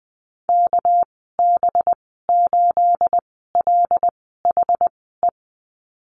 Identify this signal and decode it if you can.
Morse code, heard as a keyed tone, about 20 wpm, spaced as for 14 wpm, 700 Hz. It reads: KB8LHE
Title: 231 Lesson 37g - Letter x - Call Signs 20wpm